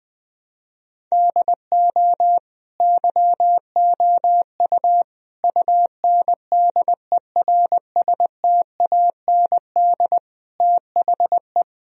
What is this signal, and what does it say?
Morse code, heard as a keyed tone, about 21 wpm, 700 Hz. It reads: DO YOU UNDERSTAND THE
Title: Sentences from Top 1000 Words - Review Part 3 - 20wpm